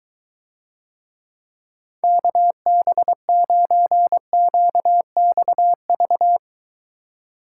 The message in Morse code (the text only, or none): KB9QXV